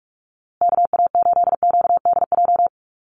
Morse code, heard as a keyed tone, about 44 wpm, 700 Hz. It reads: KA8QDJ
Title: USA Call Signs Repeated 3x 45wpm